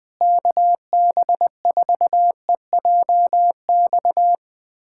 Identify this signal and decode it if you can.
Morse code, heard as a keyed tone, about 20 wpm, 700 Hz. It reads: KB4EJX
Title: USA Callsign DE Callsign - 6 Characters - 20wpm